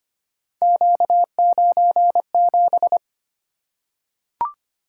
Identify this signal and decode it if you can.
Morse code, heard as a keyed tone, about 25 wpm, 700 Hz. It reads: Q97
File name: Three Letter-Number Combinations - 25wpm